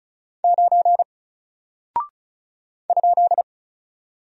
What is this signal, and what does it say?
Morse code, heard as a keyed tone, about 36 wpm, 700 Hz. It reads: 9 ?
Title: Single Character (Letter:Number:Punctuation:Prosign) Alt 35wpm